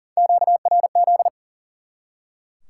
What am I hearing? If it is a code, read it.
Morse code, heard as a keyed tone, about 40 wpm, 700 Hz. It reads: QRZ
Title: Top 90 QSO Elements 40wpm